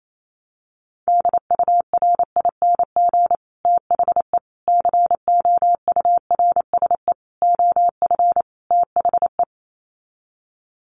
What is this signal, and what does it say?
Morse code, heard as a keyed tone, about 28 wpm, 700 Hz. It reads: DURING THE COURSE OF THE
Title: Sets of 5 Words Encore 28wpm